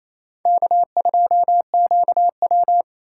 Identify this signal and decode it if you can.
Morse code, heard as a keyed tone, about 35 wpm, 700 Hz. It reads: K2QW
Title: Speed Racing - USA 1x2 Call Signs 28wpm